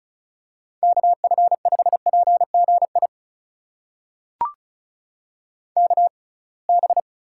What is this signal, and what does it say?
Morse code, heard as a keyed tone, about 34 wpm, spaced as for 26 wpm, 700 Hz. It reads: KF5PGI KB
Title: USA Call Signs - 6 Characters - 6x Character Spacing - 35wpm